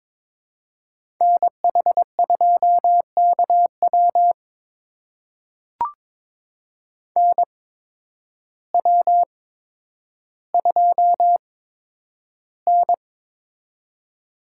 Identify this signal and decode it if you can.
Morse code, heard as a keyed tone, about 22 wpm, spaced as for 10 wpm, 700 Hz. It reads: NH2KW NW2N
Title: USA Call Signs - 8x Character Spacing - 22wpm